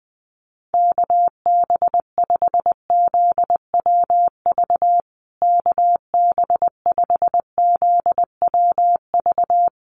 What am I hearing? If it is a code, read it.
Morse code, heard as a keyed tone, about 20 wpm, 700 Hz. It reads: KB5ZWV KB5ZWV
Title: USA Call Signs Repeated 3x 20wpm